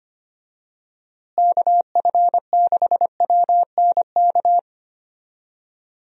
Morse code, heard as a keyed tone, about 25 wpm, 700 Hz. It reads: KF6WNK
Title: USA Call Signs - 6 Characters - Repeated 3x - 25wpm